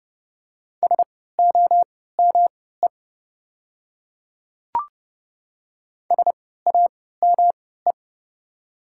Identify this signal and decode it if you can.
Morse code, heard as a keyed tone, about 30 wpm, spaced as for 17 wpm, 700 Hz. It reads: SOME SAME
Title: 3x Character Spacing - Top 300 Words - 30wpm